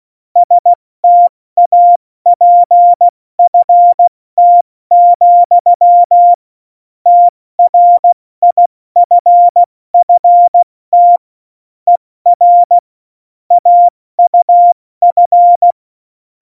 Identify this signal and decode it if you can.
Morse code, heard as a keyed tone, about 16 wpm, 700 Hz. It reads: STAPFT, TRIFFT ER AUF
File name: Morsekurs / Witze / Witz 17.Juli 2018, 70 F80